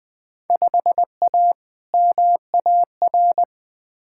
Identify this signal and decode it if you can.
Morse code, heard as a keyed tone, about 21 wpm, 700 Hz. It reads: 5A MAR
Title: Field Day Exchanges Encore - 20wpm